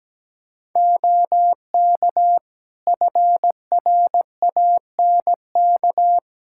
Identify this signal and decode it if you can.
Morse code, heard as a keyed tone, about 17 wpm, 700 Hz. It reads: OK FRANK
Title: Practice QSO - Exchange 3 - More Info - 17wpm